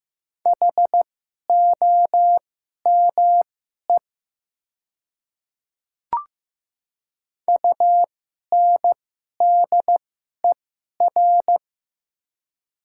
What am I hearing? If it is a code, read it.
Morse code, heard as a keyed tone, about 15 wpm, spaced as for 11 wpm, 700 Hz. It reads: HOME UNDER
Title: 2x Character Spacing - Top 200 Words - 15wpm